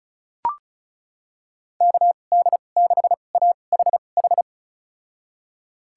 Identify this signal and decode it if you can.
Morse code, heard as a keyed tone, about 36 wpm, spaced as for 25 wpm, 700 Hz. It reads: KD6AHH
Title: USA Call Signs - 2x Character Spacing - 35wpm